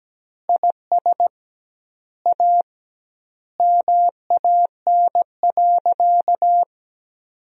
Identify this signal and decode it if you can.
Morse code, heard as a keyed tone, about 17 wpm, 700 Hz. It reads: IS A MAN.
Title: 2x Word-Spacing - Sentences from Top 500 Words - 17wpm